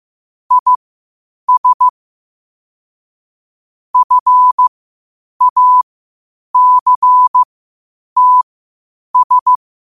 Morse code, heard as a keyed tone, about 15 wpm, spaced as for 8 wpm, 1000 Hz. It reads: IS FACTS